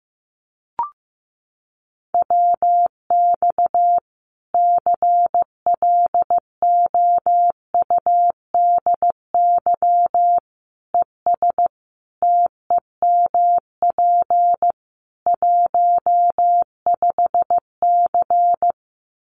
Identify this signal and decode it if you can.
Morse code, heard as a keyed tone, about 15 wpm, 700 Hz. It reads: WX CLOUDY ES TEMP 15C
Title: Common QSO Phrases 15wpm